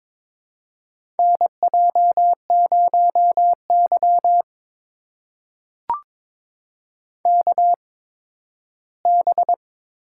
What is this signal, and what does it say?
Morse code, heard as a keyed tone, about 22 wpm, spaced as for 13 wpm, 700 Hz. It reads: NJ0Y KB
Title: USA Call Signs - 4 Characters - 8x Character Spacing - 22wpm